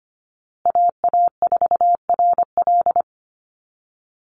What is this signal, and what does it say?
Morse code, heard as a keyed tone, about 25 wpm, 700 Hz. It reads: AA4RL
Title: USA Call Signs Repeated 2x 25wpm